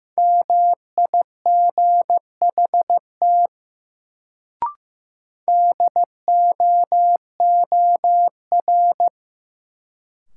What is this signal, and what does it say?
Morse code, heard as a keyed tone, about 15 wpm, 700 Hz. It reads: MIGHT DOOR
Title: Sentences from Top 300 Words - New Word Drill - 15wpm